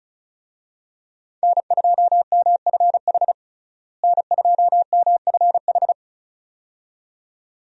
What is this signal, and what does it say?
Morse code, heard as a keyed tone, about 34 wpm, 700 Hz. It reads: N2MFH N2MFH
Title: USA Call Signs - 5 Characters - Repeated 2x - 35wpm